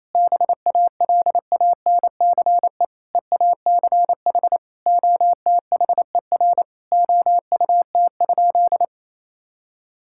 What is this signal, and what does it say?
Morse code, heard as a keyed tone, about 28 wpm, 700 Hz. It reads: BALANCE EACH OTHER OUT?
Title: Sentences from Top 1000 Words - Review Part 6 - 28wpm